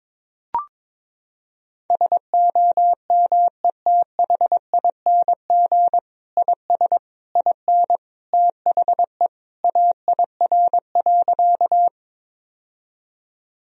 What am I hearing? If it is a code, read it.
Morse code, heard as a keyed tone, about 22 wpm, 700 Hz. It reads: SOMETHING IS IN THE AIR.
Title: Sentences from Top 1000 Words - Review Part 1 - 22wpm